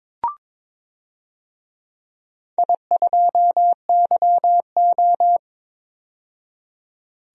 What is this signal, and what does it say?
Morse code, heard as a keyed tone, about 22 wpm, 700 Hz. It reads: I2YO
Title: DXCC-100 Call Signs - 4 Characters - 22wpm